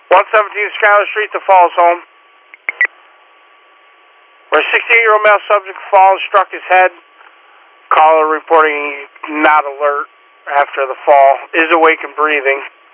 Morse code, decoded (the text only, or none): none